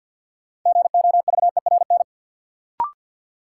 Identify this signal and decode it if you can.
Morse code, heard as a keyed tone, about 53 wpm, 700 Hz. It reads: GOVERN
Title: Sentences from Top 1000 Words - Review Part 2 - New Word Drill - 50wpm